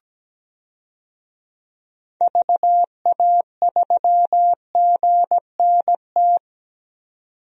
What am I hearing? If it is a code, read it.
Morse code, heard as a keyed tone, about 17 wpm, 700 Hz. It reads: VA3GNT
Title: DXCC-100 Call Signs - 6 Characters - 17wpm